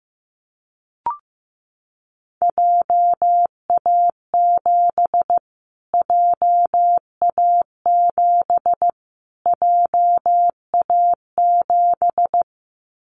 Morse code, heard as a keyed tone, about 15 wpm, 700 Hz. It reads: JA7 JA7 JA7
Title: Callsign Prefixes for Top 25 DXCC - Repeated 3x - 15wpm